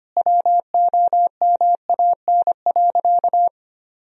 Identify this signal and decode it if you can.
Morse code, heard as a keyed tone, about 25 wpm, 700 Hz. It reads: WOMAN.
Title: Sentences from Top 1000 Words - Review Part 3 - 25wpm